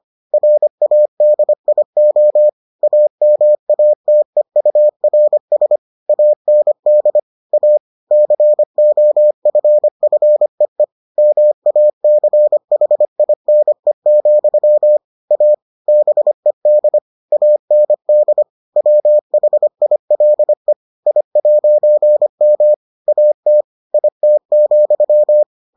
Morse code, heard as a keyed tone, about 25 wpm, 600 Hz. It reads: RADIO AMATEURS AND A COFFEE MACHINE, A BED AND WHILE I'M AT IT,